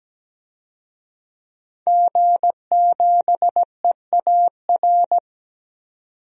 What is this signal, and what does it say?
Morse code, heard as a keyed tone, about 17 wpm, 700 Hz. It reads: G7EAR
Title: DXCC-100 Call Signs - 5 Characters - 17wpm